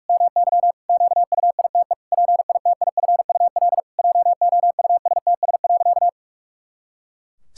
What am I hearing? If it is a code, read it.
Morse code, heard as a keyed tone, about 46 wpm, 700 Hz. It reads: MY QUITE PITIFUL JOUSTS.
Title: Pangram Challenge v2 - 45wpm